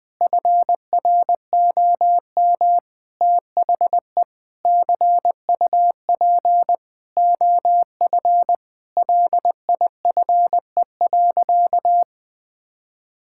Morse code, heard as a keyed tone, about 20 wpm, 700 Hz. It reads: FROM THE CUP OF LIFE.
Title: Sentences from Top 1600 Words 20wpm